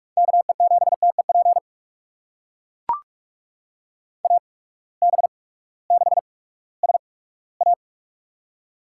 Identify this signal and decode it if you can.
Morse code, heard as a keyed tone, about 44 wpm, spaced as for 20 wpm, 700 Hz. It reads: KE7TEP AB6SA
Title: USA Call Signs - 8x Character Spacing - 45wpm